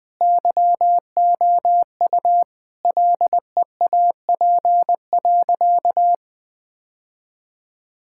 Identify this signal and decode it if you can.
Morse code, heard as a keyed tone, about 21 wpm, 700 Hz. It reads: YOU LEAP.
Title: Easy Sentences with Hints - 20wpm